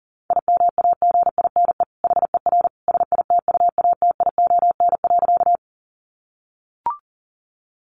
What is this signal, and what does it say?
Morse code, heard as a keyed tone, about 40 wpm, 700 Hz. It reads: IMAGINE HER SITUATION.
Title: Sentences from Top 1400 Words 40wpm